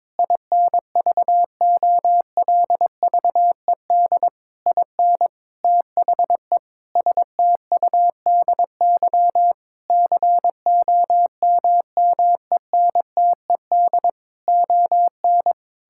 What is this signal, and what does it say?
Morse code, heard as a keyed tone, about 22 wpm, 700 Hz. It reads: INVOLVED IN THE STUDY COMMENTED ON
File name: Sentences with 7-Word Phrases - 22wpm